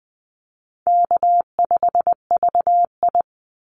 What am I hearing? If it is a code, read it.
Morse code, heard as a keyed tone, about 20 wpm, 700 Hz. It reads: K5VI
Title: Call Signs Repeated 2x - 20wpm